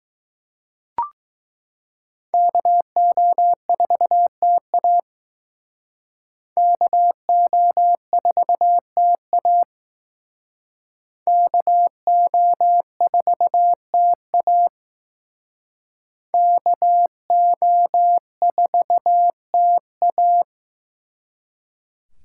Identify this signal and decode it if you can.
Morse code, heard as a keyed tone, about 19 wpm, 700 Hz. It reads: KO4TA KO4TA KO4TA KO4TA